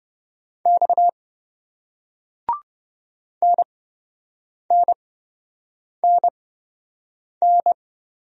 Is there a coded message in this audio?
Yes: Morse code from 0.7 to 7.7 s, heard as a keyed tone, about 28 wpm, spaced as for 7 wpm, 700 Hz. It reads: X NNNN